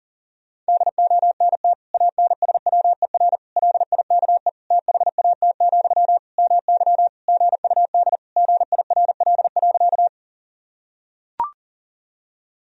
Morse code, heard as a keyed tone, about 37 wpm, 700 Hz. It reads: DONT ANSWER LIKE THAT, MY GUD GIRL.
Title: Sentences from Top 400 Words and Abbreviations 40wpm